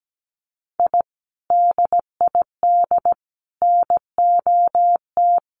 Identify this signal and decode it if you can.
Morse code, heard as a keyed tone, about 17 wpm, 700 Hz. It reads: I DID NOT
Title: Sets of 5 Words Encore 17wpm